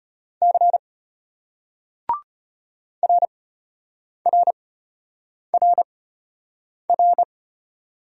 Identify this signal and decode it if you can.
Morse code, heard as a keyed tone, about 34 wpm, spaced as for 8 wpm, 700 Hz. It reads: C RRRR